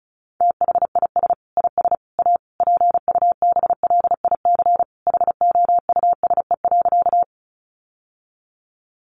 Morse code, heard as a keyed tone, about 34 wpm, 700 Hz. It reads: THIS IS A PUBLIC HOUSE.
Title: Sentences from Top 600 Words 35wpm